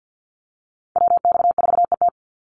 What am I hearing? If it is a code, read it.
Morse code, heard as a keyed tone, about 50 wpm, 700 Hz. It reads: WX4ET
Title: NA SOTA Chaser Callsigns 50wpm